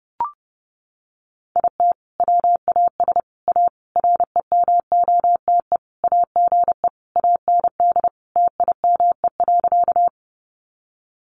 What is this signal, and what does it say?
Morse code, heard as a keyed tone, about 30 wpm, 700 Hz. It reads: IT WAS A REMOTE AGE AND TIME.
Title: Sentences from Top 1300 Words 30wpm